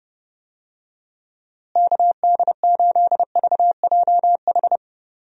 Morse code, heard as a keyed tone, about 32 wpm, 700 Hz. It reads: KD8VJH
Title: USA Call Signs - 6 Characters - Repeated 2x - 30wpm